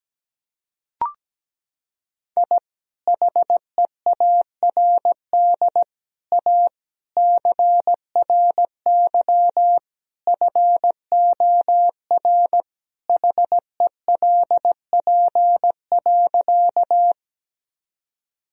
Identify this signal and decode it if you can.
Morse code, heard as a keyed tone, about 17 wpm, 700 Hz. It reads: I HEARD A CRY FOR HELP.
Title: Sentences from Top 1000 Words - Review Part 2 - 17wpm